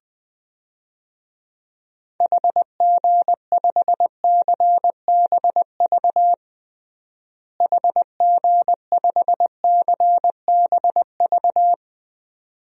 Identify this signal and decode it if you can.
Morse code, heard as a keyed tone, about 21 wpm, 700 Hz. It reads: HG5CBV HG5CBV
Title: DXCC-100 Call Signs - 6 Characters - Repeated 2x - 20wpm